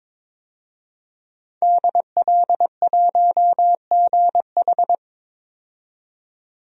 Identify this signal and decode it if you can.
Morse code, heard as a keyed tone, about 22 wpm, 700 Hz. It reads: DL1GH
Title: DXCC-100 Call Signs - 5 Characters - 22wpm